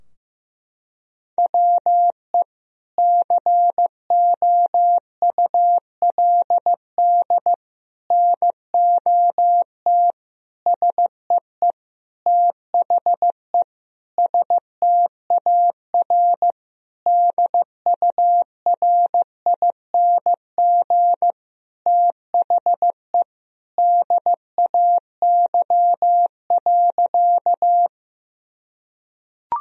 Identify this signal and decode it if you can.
Morse code, heard as a keyed tone, about 15 wpm, 700 Hz. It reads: WE COULD NOT SEE THE STAR DURING THE DAY.
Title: Sentences from Top 1000 Words - Review Part 2 - 15wpm